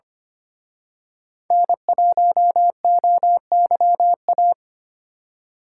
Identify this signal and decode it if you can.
Morse code, heard as a keyed tone, about 29 wpm, 700 Hz. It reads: N1OYA
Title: Speed Racing - USA 1x3 Call Signs 25wpm